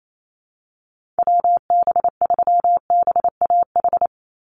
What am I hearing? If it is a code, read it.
Morse code, heard as a keyed tone, about 28 wpm, 700 Hz. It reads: WB3BAH
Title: USA Call Signs 28wpm